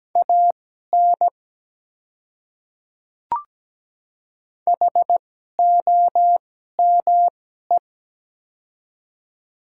Morse code, heard as a keyed tone, about 17 wpm, spaced as for 12 wpm, 700 Hz. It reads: AN HOME
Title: 2x Character Spacing - Top 300 Words - 17wpm